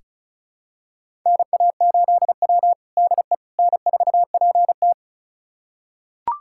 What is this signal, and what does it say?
Morse code, heard as a keyed tone, about 34 wpm, 700 Hz. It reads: NA8W DE N4PT
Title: USA Callsign DE Callsign - 4 Characters - 35wpm